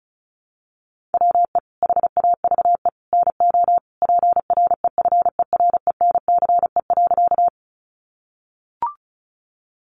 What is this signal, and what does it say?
Morse code, heard as a keyed tone, about 34 wpm, 700 Hz. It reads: WE HAVE NO PREFERENCE.
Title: Sentences from Top 1500 Words 35wpm